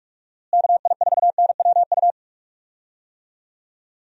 Morse code, heard as a keyed tone, about 47 wpm, 700 Hz. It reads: KI4NWU
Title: USA Call Signs - 6 Characters - Repeated 3x - 45wpm